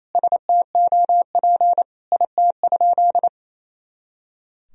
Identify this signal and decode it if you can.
Morse code, heard as a keyed tone, about 28 wpm, 700 Hz. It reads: STOP IT?
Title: Sentences from Top 300 Words and Abbreviations 28wpm